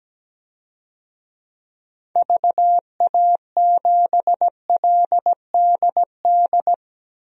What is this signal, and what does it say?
Morse code, heard as a keyed tone, about 17 wpm, 700 Hz. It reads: VA7LDD
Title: DXCC-100 Call Signs - 6 Characters - 17wpm